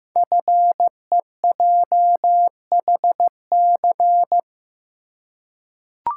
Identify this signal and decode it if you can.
Morse code, heard as a keyed tone, about 15 wpm, 700 Hz. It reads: FEJHC
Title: Code Groups - 5 Letters - 15wpm